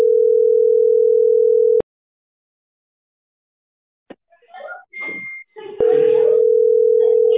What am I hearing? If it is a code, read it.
Ringback tone (North America), 440 + 480 Hz.